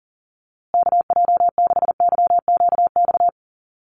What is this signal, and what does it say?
Morse code, heard as a keyed tone, about 37 wpm, 700 Hz. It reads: KJ6YQX
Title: USA Call Signs Repeated 3x 40wpm